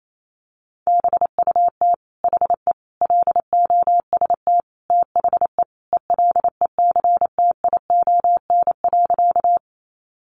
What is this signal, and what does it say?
Morse code, heard as a keyed tone, about 28 wpm, 700 Hz. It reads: BUT HE LOST THE ELECTION.
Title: Sentences from Top 1500 Words 28wpm